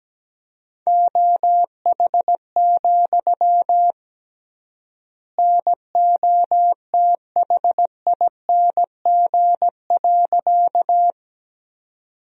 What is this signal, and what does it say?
Morse code, heard as a keyed tone, about 17 wpm, 700 Hz. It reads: OH, NOTHING.
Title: 3x Word-Spacing - Sentences from Top 500 Words - 17wpm